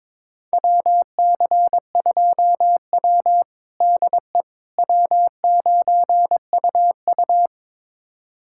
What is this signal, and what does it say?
Morse code, heard as a keyed tone, about 22 wpm, 700 Hz. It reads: WC2W DE W9UU
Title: USA Callsign DE Callsign - 4 Characters - 22wpm